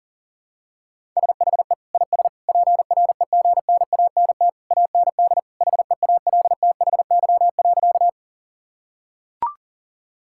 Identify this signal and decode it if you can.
Morse code, heard as a keyed tone, about 37 wpm, 700 Hz. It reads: SHE IS PREGNANT AND HEALTHY.
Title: Sentences from Top 1000 Words - Review Part 6 - 40wpm